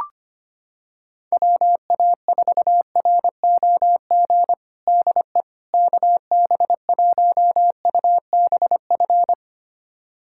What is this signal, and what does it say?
Morse code, heard as a keyed tone, about 25 wpm, 700 Hz. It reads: WA4ROG DE KB1UBF
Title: USA Callsign DE Callsign - 6 Characters - 25wpm